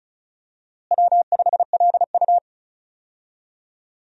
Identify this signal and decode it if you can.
Morse code, heard as a keyed tone, about 36 wpm, 700 Hz. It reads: W5LU
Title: USA Call Signs - 4 Characters - Repeated 3x - 35wpm